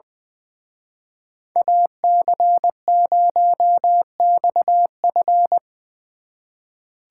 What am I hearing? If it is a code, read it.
Morse code, heard as a keyed tone, about 23 wpm, 700 Hz. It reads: AC0XF
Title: Speed Racing - USA 2x2 Call Signs 20wpm